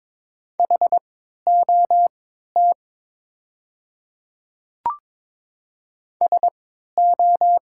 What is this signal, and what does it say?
Morse code, heard as a keyed tone, about 22 wpm, spaced as for 12 wpm, 700 Hz. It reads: HOT SO
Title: 3x Character Spacing - Top 100 Words - 22wpm